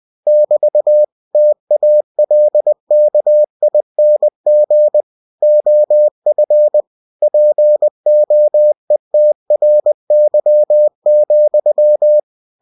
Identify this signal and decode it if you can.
Morse code, heard as a keyed tone, about 20 wpm, 600 Hz. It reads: <BT> TALKING OF POETRY,
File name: cw-books / winnie-the-pooh / winnie-20wpm-page-0068